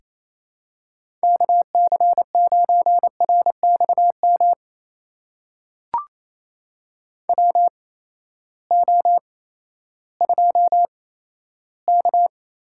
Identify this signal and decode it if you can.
Morse code, heard as a keyed tone, about 28 wpm, spaced as for 14 wpm, 700 Hz. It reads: KC9RXM WO2K
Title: USA Call Signs - 8x Character Spacing - 28wpm